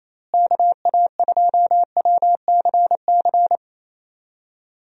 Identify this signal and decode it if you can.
Morse code, heard as a keyed tone, about 28 wpm, 700 Hz. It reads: KA2WCC
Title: USA Call Signs - 6 Characters - Repeated 3x - 28wpm